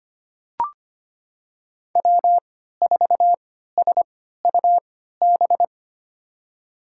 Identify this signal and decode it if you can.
Morse code, heard as a keyed tone, about 25 wpm, spaced as for 14 wpm, 700 Hz. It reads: W4SUB
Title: USA Call Signs - 3x Character Spacing - 25wpm